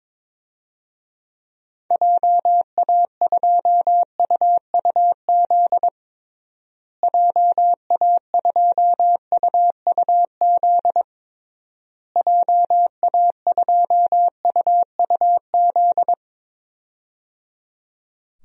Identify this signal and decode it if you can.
Morse code, heard as a keyed tone, about 22 wpm, 700 Hz. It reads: JA2UUZ JA2UUZ JA2UUZ
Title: DXCC-100 Call Signs - 6 Characters - Repeated 3x - 22wpm